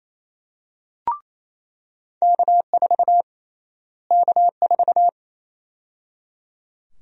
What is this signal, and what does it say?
Morse code, heard as a keyed tone, about 27 wpm, 700 Hz. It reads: K4 K4
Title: USA Call Sign Prefixes - Repeated 2x - 28wpm